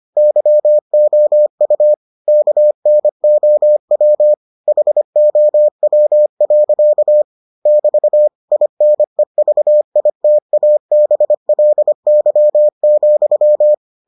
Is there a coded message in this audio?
Yes: Morse code from 0.2 to 13.7 s, heard as a keyed tone, about 25 wpm, 600 Hz. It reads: YOU KNOW HOW. <BT> INEVITABLY,